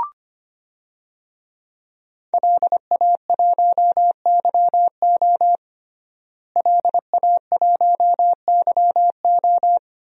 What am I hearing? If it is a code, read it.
Morse code, heard as a keyed tone, about 25 wpm, 700 Hz. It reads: LA1YO LA1YO
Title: DXCC-100 Call Signs - 5 Characters - Repeated 3x - 25wpm